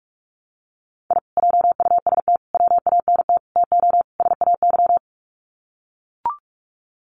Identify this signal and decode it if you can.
Morse code, heard as a keyed tone, about 44 wpm, 700 Hz. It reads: I JUST WANT TO SAY
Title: Sets of 5 Words Encore 45wpm